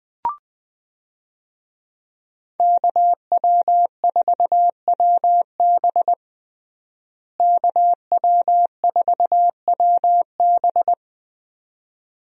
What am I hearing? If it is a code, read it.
Morse code, heard as a keyed tone, about 20 wpm, 700 Hz. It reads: KW4WB KW4WB
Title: USA Call Signs - 5 Characters - Repeated 2x - 20wpm